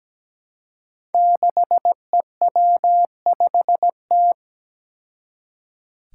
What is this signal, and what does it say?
Morse code, heard as a keyed tone, about 17 wpm, 700 Hz. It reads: 6EW5T
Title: Five Challenging Letter-Number Combinations 17wpm